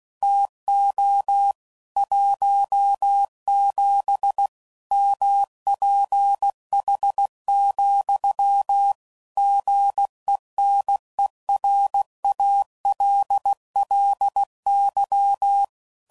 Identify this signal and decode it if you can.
Morse code, heard as a keyed tone, about 16 wpm, 800 Hz. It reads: TO 17 MPH, GENERALLY
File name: CW Code Groups @15WPM / Weather Sets / Weather 4 21